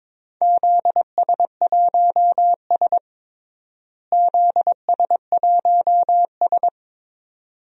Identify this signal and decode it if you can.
Morse code, heard as a keyed tone, about 22 wpm, 700 Hz. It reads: ZS1S ZS1S
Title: DXCC-100 Call Signs - 4 Characters - Repeated 2x - 22wpm